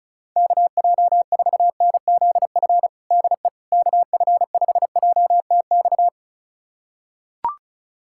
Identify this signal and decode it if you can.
Morse code, heard as a keyed tone, about 36 wpm, 700 Hz. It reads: KJ4NZF DE KF5JTX
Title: USA Callsign DE Callsign - 6 Characters - 35wpm